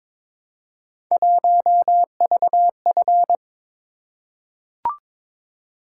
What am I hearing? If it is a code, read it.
Morse code, heard as a keyed tone, about 22 wpm, 700 Hz. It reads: 1VF
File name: practice / 22wpm / Three Letter-Number Combinations - 22wpm